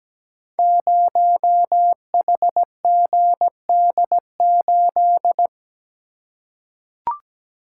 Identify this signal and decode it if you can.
Morse code, heard as a keyed tone, about 17 wpm, 700 Hz. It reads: 0HGD8
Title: Code Groups - 5 Letter:Numbers - 17wpm